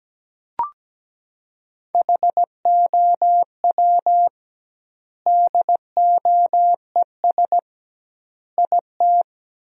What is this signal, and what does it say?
Morse code, heard as a keyed tone, about 17 wpm, 700 Hz. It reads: HOW DOES IT
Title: 2x Word-Spacing - Sentences from Top 400 Words - 17wpm